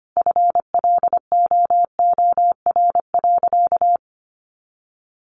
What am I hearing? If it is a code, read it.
Morse code, heard as a keyed tone, about 25 wpm, 700 Hz. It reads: FLOOR.
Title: Additional Easy Phrases 25wpm